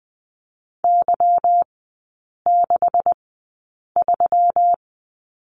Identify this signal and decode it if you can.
Morse code, heard as a keyed tone, about 20 wpm, spaced as for 8 wpm, 700 Hz. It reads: Y63
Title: 159 Lesson 26e - Letter g - Three Characters 20wpm